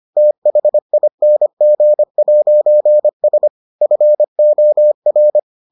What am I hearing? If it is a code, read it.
Morse code, heard as a keyed tone, about 25 wpm, 600 Hz. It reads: THING'S FOR